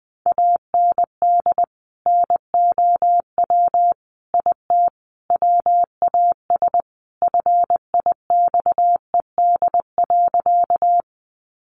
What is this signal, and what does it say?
Morse code, heard as a keyed tone, about 19 wpm, 700 Hz. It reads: AND NOW IT WAS FIXED.
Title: Sentences from Top 1100 Words 20wpm